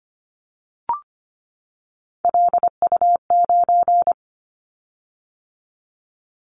Morse code, heard as a keyed tone, about 25 wpm, 700 Hz. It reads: LU9
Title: Callsign Prefixes for Top 100 DXCC - 25wpm